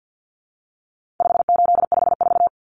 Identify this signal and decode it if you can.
Morse code, heard as a keyed tone, about 49 wpm, 700 Hz. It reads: 5854
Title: Common Cut Numbers - 4-Digit 50wpm